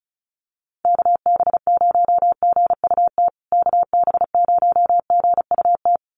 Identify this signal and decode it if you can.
Morse code, heard as a keyed tone, about 34 wpm, 700 Hz. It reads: KB0GUT KB0GUT
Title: USA Call Signs Repeated 2x 35wpm